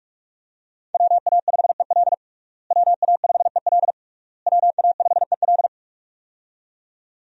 Morse code, heard as a keyed tone, about 46 wpm, 700 Hz. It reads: WA5EL WA5EL WA5EL
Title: USA Call Signs - 5 Characters - Repeated 3x - 45wpm